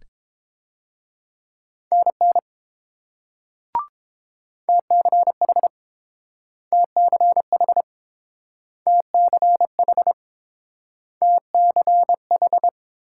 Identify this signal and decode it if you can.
Morse code, heard as a keyed tone, about 30 wpm, 700 Hz. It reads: NN TCH TCH TCH TCH